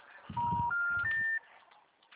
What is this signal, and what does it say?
Special information tone.